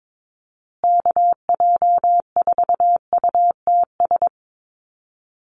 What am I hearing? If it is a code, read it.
Morse code, heard as a keyed tone, about 22 wpm, 700 Hz. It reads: KJ4UTS
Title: USA Call Signs Repeated 3x 22wpm